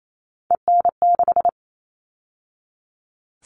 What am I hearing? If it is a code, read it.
Morse code, heard as a keyed tone, about 28 wpm, 700 Hz. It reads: EN6
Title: All Cut Numbers - 3-Digit 28wpm